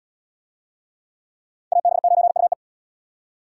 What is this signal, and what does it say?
Morse code, heard as a keyed tone, about 72 wpm, 690 Hz. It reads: AB9UE